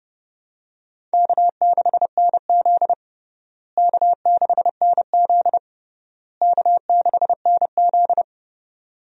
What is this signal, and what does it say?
Morse code, heard as a keyed tone, about 30 wpm, 700 Hz. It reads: K6NZ K6NZ K6NZ
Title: USA Call Signs - 4 Characters - Repeated 3x - 30wpm